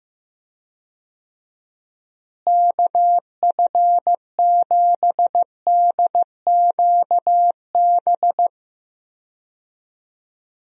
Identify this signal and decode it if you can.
Morse code, heard as a keyed tone, about 15 wpm, 700 Hz. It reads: KF7DQB
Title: USA Call Signs - 6 Characters - Repeated 2x - 15wpm